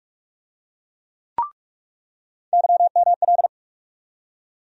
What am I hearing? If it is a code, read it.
Morse code, heard as a keyed tone, about 46 wpm, 700 Hz. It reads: YML
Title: USA Call Sign Postfixes - 3 Letters 45wpm